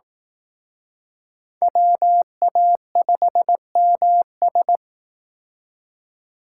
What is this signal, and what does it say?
Morse code, heard as a keyed tone, about 18 wpm, 700 Hz. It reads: WA5MS
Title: Speed Racing - USA 2x2 Call Signs 15wpm